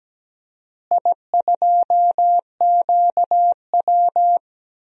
Morse code, heard as a keyed tone, about 17 wpm, 700 Hz. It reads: I2QW
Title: DXCC-100 Call Signs - 4 Characters - Repeated 3x - 17wpm